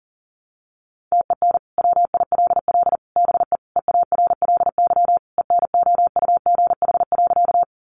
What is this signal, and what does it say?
Morse code, heard as a keyed tone, about 40 wpm, 700 Hz. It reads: TEN WILL BE EARLY ENOUGH.
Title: Sentences from Top 400 Words 40wpm